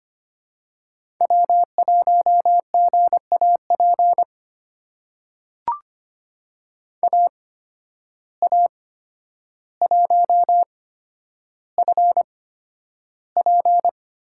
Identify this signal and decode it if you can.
Morse code, heard as a keyed tone, about 25 wpm, spaced as for 11 wpm, 700 Hz. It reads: W1GAP AA1FP
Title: USA Call Signs - 5 Characters - 8x Character Spacing - 25wpm